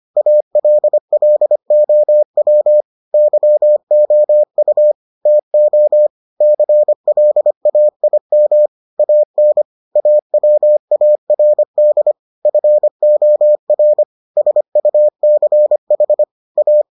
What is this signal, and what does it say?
Morse code, heard as a keyed tone, about 25 wpm, 600 Hz. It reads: ALLOW YOU TO CLAIM AN AWARD FOR SUCH A